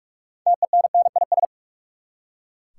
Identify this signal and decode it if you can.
Morse code, heard as a keyed tone, about 42 wpm, 700 Hz. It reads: TENNIS
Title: Six Letter Words - 45wpm